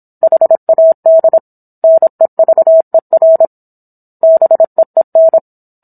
Morse code, heard as a keyed tone, about 27 wpm, 650 Hz. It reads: HAD NEVER BEEN